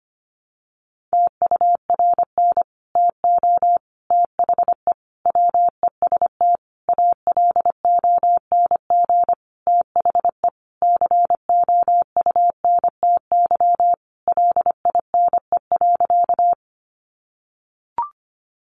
Morse code, heard as a keyed tone, about 25 wpm, 700 Hz. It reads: TURN TO THE WEST ALONG THE COUNTY LINE.
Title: Sentences from Top 1200 Words 25wpm